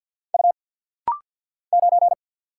Morse code, heard as a keyed tone, about 49 wpm, 700 Hz. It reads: U 9
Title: Single Letter-Number - Rapid-Fire 50wpm